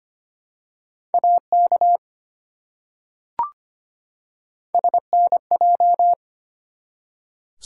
Morse code, heard as a keyed tone, about 25 wpm, 700 Hz. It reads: AK SNJ
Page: morse-code-ninja-practice-25wpm